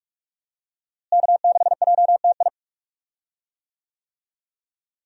Morse code, heard as a keyed tone, about 46 wpm, 700 Hz. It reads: K6JTI
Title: USA Call Signs - 5 Characters - Repeated 3x - 45wpm